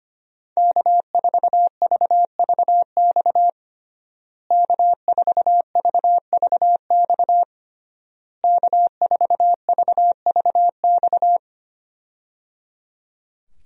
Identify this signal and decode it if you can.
Morse code, heard as a keyed tone, about 25 wpm, 700 Hz. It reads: K4VVX K4VVX K4VVX
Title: USA Call Signs - 5 Characters - Repeated 3x - 25wpm